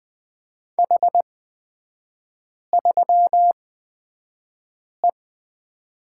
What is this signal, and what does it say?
Morse code, heard as a keyed tone, about 20 wpm, spaced as for 5 wpm, 700 Hz. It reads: H3E
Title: 255 Lesson 41e - Period - Three Characters 20wpm